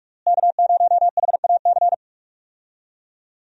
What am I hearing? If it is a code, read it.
Morse code, heard as a keyed tone, about 42 wpm, 700 Hz. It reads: K0HAC
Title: USA Call Signs - 5 Characters - Repeated 3x - 45wpm